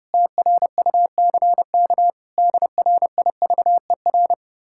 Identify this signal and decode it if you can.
Morse code, heard as a keyed tone, about 33 wpm, 700 Hz. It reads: TRUCK DRIVER